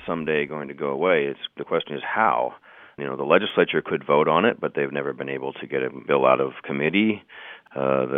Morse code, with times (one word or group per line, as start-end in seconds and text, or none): none